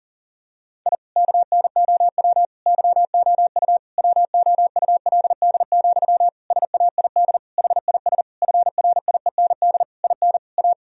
Morse code, heard as a keyed tone, about 40 wpm, 700 Hz. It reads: I KNOW YOU WOULD, SAID HIS FRIEND IN A
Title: Sentences from Top 1000 Words - Review Part 1 - 40wpm